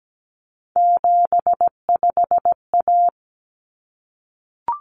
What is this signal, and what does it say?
Morse code, heard as a keyed tone, about 17 wpm, 700 Hz. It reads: 75A
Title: Common Cut Numbers - 3-Digit 17wpm